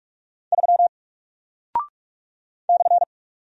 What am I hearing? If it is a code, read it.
Morse code, heard as a keyed tone, about 46 wpm, 700 Hz. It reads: 3 /
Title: Single Character (Letter:Number:Punctuation:Prosign) 45wpm